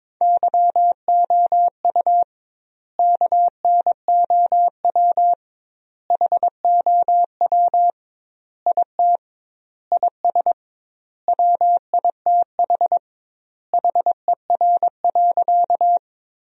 Morse code, heard as a keyed tone, about 22 wpm, 700 Hz. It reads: YOU KNOW HOW IT IS WITH HER.
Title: 2x Word-Spacing - Sentences from Top 100 Words - 22wpm